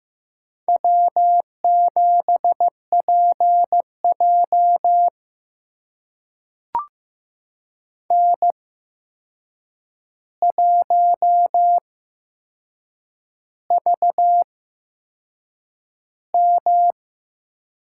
Morse code, heard as a keyed tone, about 15 wpm, spaced as for 6 wpm, 700 Hz. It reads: W7PJ N1VM
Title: USA Call Signs - 4 Characters - 8x Character Spacing - 15wpm